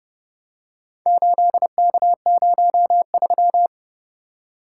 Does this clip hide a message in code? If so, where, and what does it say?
Yes: Morse code from 1.1 to 3.7 s, heard as a keyed tone, about 28 wpm, 700 Hz. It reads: 8K03